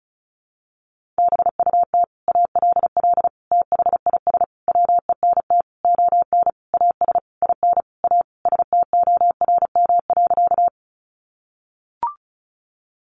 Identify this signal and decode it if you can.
Morse code, heard as a keyed tone, about 34 wpm, 700 Hz. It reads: BUT ALL THIS WENT ON AS IN A STORM.
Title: Sentences from Top 1500 Words 35wpm